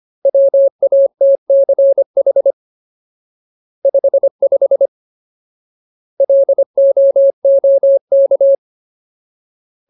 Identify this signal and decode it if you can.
Morse code, heard as a keyed tone, about 25 wpm, 550 Hz. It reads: WATCH 55 LOOK
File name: Ultimate Word List / 25WPM Ultimate